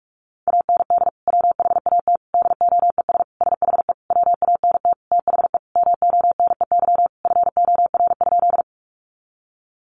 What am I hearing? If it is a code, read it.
Morse code, heard as a keyed tone, about 44 wpm, 700 Hz. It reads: AND WHAT DOES SHE WANT THE MONEY FOR?
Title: Sentences from Top 400 Words 45wpm